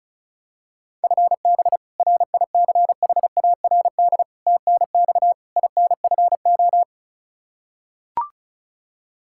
Morse code, heard as a keyed tone, about 36 wpm, 700 Hz. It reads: FB RICHARD TNX INFO
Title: Practice QSO - Exchange 3 - More Info - 35wpm